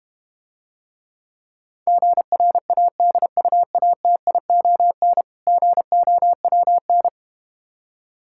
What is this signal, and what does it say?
Morse code, heard as a keyed tone, about 31 wpm, 700 Hz. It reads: GRADUATION GOWN
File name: practice / 25wpm / Difficult Words - Speed-Racing with Hints - 25wpm